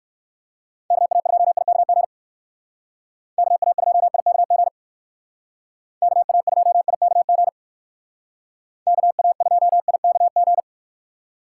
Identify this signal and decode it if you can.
Morse code, heard as a keyed tone, about 56 wpm, 700 Hz. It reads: KA2IKG KA2IKG KA2IKG KA2IKG
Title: Speed Racing - USA 2x3 Call Signs 45wpm